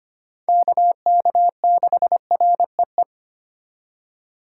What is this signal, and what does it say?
Morse code, heard as a keyed tone, about 32 wpm, 700 Hz. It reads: KK6REE